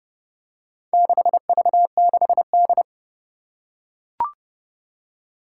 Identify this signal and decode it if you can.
Morse code, heard as a keyed tone, about 30 wpm, 700 Hz. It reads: 6V6D
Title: Four Challenging Letter-Number Combinations 30wpm